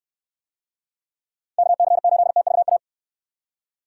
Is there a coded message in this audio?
Yes: Morse code from 1.6 to 2.8 s, heard as a keyed tone, about 69 wpm, 700 Hz. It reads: KC8TVA